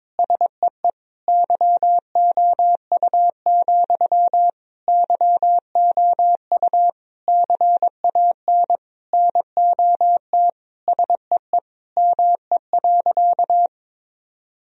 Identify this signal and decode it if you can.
Morse code, heard as a keyed tone, about 22 wpm, 700 Hz. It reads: SEE YOU, YOU CAN NOT SEE ME.
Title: Sentences from Top 1000 Words - Review Part 3 - 22wpm